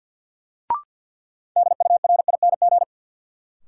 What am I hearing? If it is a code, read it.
Morse code, heard as a keyed tone, about 49 wpm, 700 Hz. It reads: DURING
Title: Sentences from Top 700 Words - New Word Drill - 50wpm